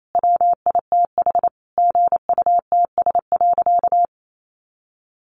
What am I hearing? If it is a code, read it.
Morse code, heard as a keyed tone, about 28 wpm, 700 Hz. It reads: WITH GUTS.
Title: A Sound Of Thunder 28wpm